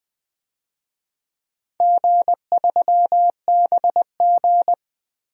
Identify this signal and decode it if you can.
Morse code, heard as a keyed tone, about 21 wpm, 700 Hz. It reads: G3BG
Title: DXCC-100 Call Signs - 4 Characters - Repeated 2x - 20wpm